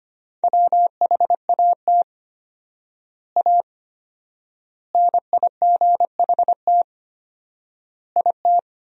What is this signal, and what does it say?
Morse code, heard as a keyed tone, about 25 wpm, 700 Hz. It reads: WHAT A NIGHT IT
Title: 4x Word-Spacing - Sentences from Top 300 Words - 25wpm